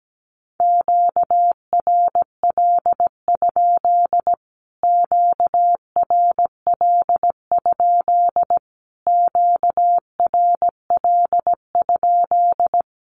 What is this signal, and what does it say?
Morse code, heard as a keyed tone, about 17 wpm, 700 Hz. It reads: QRL? QRL? QRL?
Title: Q-Codes Repeated 3x 17wpm